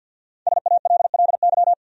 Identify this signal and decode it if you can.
Morse code, heard as a keyed tone, about 53 wpm, 700 Hz. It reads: SALLY
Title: USA Common Names - 50wpm